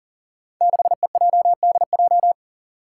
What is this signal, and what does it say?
Morse code, heard as a keyed tone, about 40 wpm, 700 Hz. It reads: 6EJDJ
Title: Five Challenging Letter-Number Combinations 40wpm